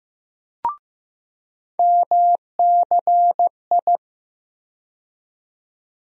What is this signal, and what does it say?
Morse code, heard as a keyed tone, about 15 wpm, 700 Hz. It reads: MCI
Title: USA Call Sign Postfixes - 3 Letters 15wpm